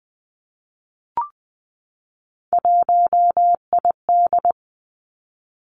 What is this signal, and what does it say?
Morse code, heard as a keyed tone, about 19 wpm, 700 Hz. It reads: 1ID
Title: Three Challenging Letter-Number Combinations 20wpm